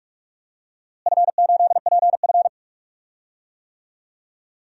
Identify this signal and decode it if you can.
Morse code, heard as a keyed tone, about 47 wpm, 700 Hz. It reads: F8PF
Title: DXCC-100 Call Signs - 4 Characters - Repeated 2x - 45wpm